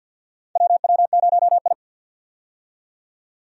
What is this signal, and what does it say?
Morse code, heard as a keyed tone, about 45 wpm, 700 Hz. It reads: WW0I
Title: USA Call Signs - 4 Characters - 50wpm